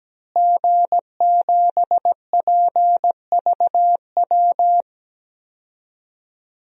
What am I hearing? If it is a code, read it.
Morse code, heard as a keyed tone, about 17 wpm, 700 Hz. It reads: G7PVW
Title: DXCC-100 Call Signs - 5 Characters - Repeated 3x - 17wpm